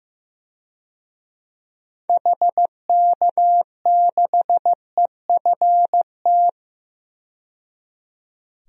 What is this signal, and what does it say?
Morse code, heard as a keyed tone, about 15 wpm, 700 Hz. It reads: HK6EFT
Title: DXCC-100 Call Signs - 6 Characters - 15wpm